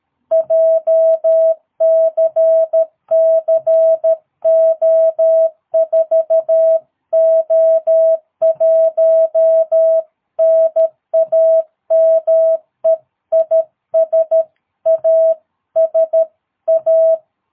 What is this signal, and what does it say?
Morse code, heard as a keyed tone, about 13 wpm, 660 Hz. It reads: JCC O4O1 NAME IS ASA
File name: jh8cbh / 2022cw